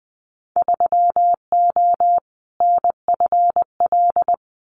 Morse code, heard as a keyed tone, about 20 wpm, 700 Hz. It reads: 3O NFL
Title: Winter Field Day Exchanges 20wpm